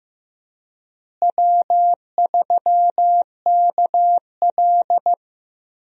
Morse code, heard as a keyed tone, about 15 wpm, 700 Hz. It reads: W3KL
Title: USA Call Signs - 4 Characters - Repeated 3x - 15wpm